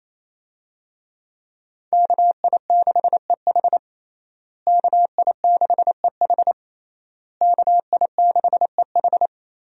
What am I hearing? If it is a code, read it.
Morse code, heard as a keyed tone, about 27 wpm, 700 Hz. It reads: KI6EH KI6EH KI6EH
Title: USA Call Signs - 5 Characters - Repeated 3x - 28wpm